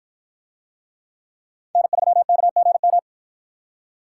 Morse code, heard as a keyed tone, about 53 wpm, 700 Hz. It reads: N3KGM